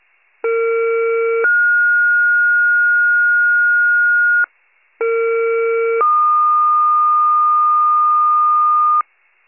Two-tone page, heard as a one-second tone, then a three-second tone, about 460 then 1430 Hz, about 460 then 1220 Hz, with noise. It is heard from 0.4 to 9.0 s.